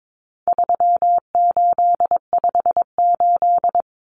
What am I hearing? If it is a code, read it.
Morse code, heard as a keyed tone, about 22 wpm, 700 Hz. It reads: 3858
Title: Common Cut Numbers - 4-Digit 22wpm